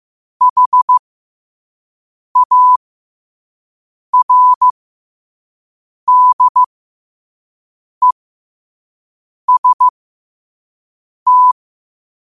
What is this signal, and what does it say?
Morse code, heard as a keyed tone, about 15 wpm, spaced as for 5 wpm, 1000 Hz. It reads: HARDEST
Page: qotd-at-5-wpm